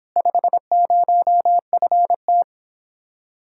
Morse code, heard as a keyed tone, about 26 wpm, 700 Hz. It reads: 50FT